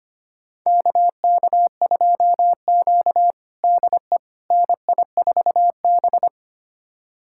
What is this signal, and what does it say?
Morse code, heard as a keyed tone, about 25 wpm, 700 Hz. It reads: KK2Q DE NI4B
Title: USA Callsign DE Callsign - 4 Characters - 25wpm